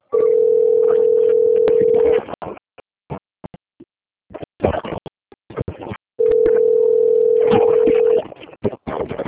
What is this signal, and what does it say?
Ringback tone (North America), 440 + 480 Hz.
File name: out- - - - - .4002